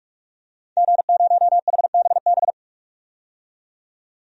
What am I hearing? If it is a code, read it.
Morse code, heard as a keyed tone, about 47 wpm, 700 Hz. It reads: G0HBB
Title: DXCC-100 Call Signs - 5 Characters - 45wpm